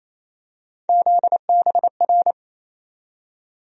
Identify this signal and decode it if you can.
Morse code, heard as a keyed tone, about 28 wpm, 700 Hz. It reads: ZBR
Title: USA Call Sign Postfixes - 3 Letters 28wpm